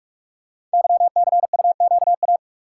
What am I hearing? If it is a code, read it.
Morse code, heard as a keyed tone, about 44 wpm, 700 Hz. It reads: YCUQA
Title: Code Groups - 5 Letters - 45wpm